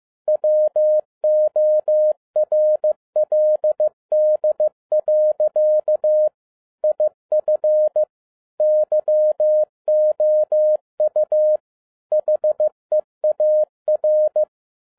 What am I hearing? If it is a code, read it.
Morse code, heard as a keyed tone, about 15 wpm, 610 Hz. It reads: WORLD. IF YOU HEAR
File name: 2011-07-mp3 / n02 JF2PDA 15wpm